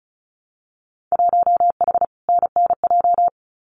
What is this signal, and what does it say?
Morse code, heard as a keyed tone, about 36 wpm, 700 Hz. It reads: 1H NNJ
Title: Winter Field Day Exchanges 35wpm